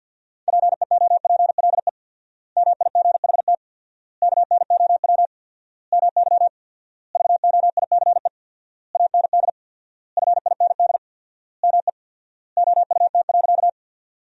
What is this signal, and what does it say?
Morse code, heard as a keyed tone, about 50 wpm, 700 Hz. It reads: PEOPLE MIGHT KNOW MY VOICE AND FIND ME OUT.
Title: 4x Word-Spacing - Sentences from Top 500 Words - 50wpm